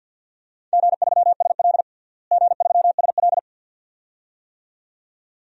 Morse code, heard as a keyed tone, about 49 wpm, 700 Hz. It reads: G3SL G3SL
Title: DXCC-100 Call Signs - 4 Characters - Repeated 2x - 50wpm